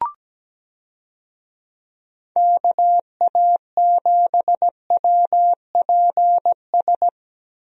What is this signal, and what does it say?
Morse code, heard as a keyed tone, about 17 wpm, 700 Hz. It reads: KA7WPS
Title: USA Call Signs - 6 Characters - Repeated 2x - 17wpm